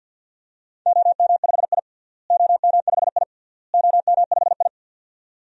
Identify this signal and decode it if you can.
Morse code, heard as a keyed tone, about 50 wpm, 700 Hz. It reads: OM5I OM5I OM5I
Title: DXCC-100 Call Signs - 4 Characters - Repeated 3x - 50wpm